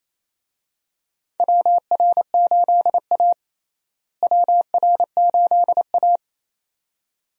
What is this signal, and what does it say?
Morse code, heard as a keyed tone, about 27 wpm, 700 Hz. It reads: WR8A WR8A
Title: USA Call Signs - 4 Characters - Repeated 2x - 28wpm